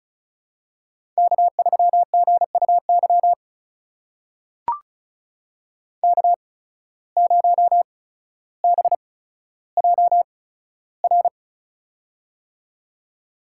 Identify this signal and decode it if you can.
Morse code, heard as a keyed tone, about 36 wpm, spaced as for 15 wpm, 700 Hz. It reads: K3GUY K0BJR
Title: USA Call Signs - 5 Characters - 8x Character Spacing - 35wpm